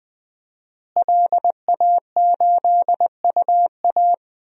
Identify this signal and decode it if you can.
Morse code, heard as a keyed tone, about 19 wpm, 700 Hz. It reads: LA8UA
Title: DXCC-100 Call Signs - 5 Characters - Repeated 3x - 20wpm